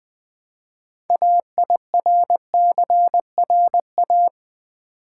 Morse code, heard as a keyed tone, about 20 wpm, 700 Hz. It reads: AIRCRA
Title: Build A Word - 3 Letters At A Time - 8-Letter Words - 20wpm